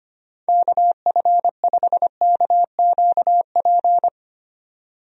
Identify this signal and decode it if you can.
Morse code, heard as a keyed tone, about 25 wpm, 700 Hz. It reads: KF5KQP
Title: USA Call Signs - 6 Characters - 25wpm